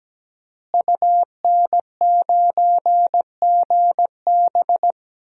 Morse code, heard as a keyed tone, about 17 wpm, 700 Hz. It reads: UN9GB
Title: DXCC-100 Call Signs - 5 Characters - Repeated 2x - 17wpm